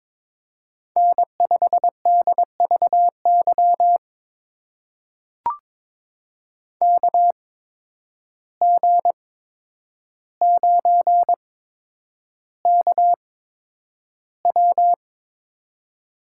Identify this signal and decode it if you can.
Morse code, heard as a keyed tone, about 21 wpm, spaced as for 9 wpm, 700 Hz. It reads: N5DVY KG9KW
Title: USA Call Signs - 5 Characters - 8x Character Spacing - 22wpm